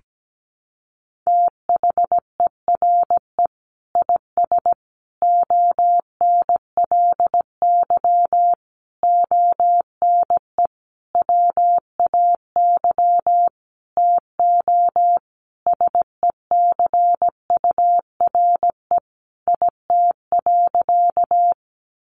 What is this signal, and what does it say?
Morse code, heard as a keyed tone, about 17 wpm, 700 Hz. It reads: THERE IS ONLY ONE WAY TO SECURE IT.
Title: Sentences from Top 1200 Words 17wpm